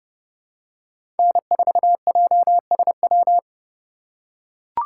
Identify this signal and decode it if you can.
Morse code, heard as a keyed tone, about 30 wpm, 700 Hz. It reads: N4JSW
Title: USA Call Signs - 5 Characters - 4x Character Spacing - 30wpm